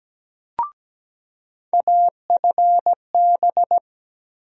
Morse code, heard as a keyed tone, about 17 wpm, 700 Hz. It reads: AFB
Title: USA Call Sign Postfixes - 3 Letters 17wpm